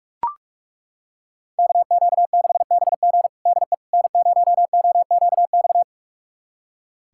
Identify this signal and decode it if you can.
Morse code, heard as a keyed tone, about 41 wpm, 700 Hz. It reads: KQ6BG DE N0OQX
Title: USA Callsign DE Callsign - 5 Characters - 45wpm